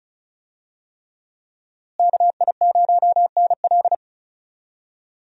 Morse code, heard as a keyed tone, about 36 wpm, 700 Hz. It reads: KI0NL
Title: USA Call Signs - 5 Characters - Repeated 3x - 35wpm